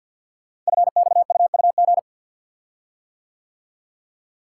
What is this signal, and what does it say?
Morse code, heard as a keyed tone, about 50 wpm, 700 Hz. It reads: FXUUG
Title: Code Groups - 5 Letters - 50wpm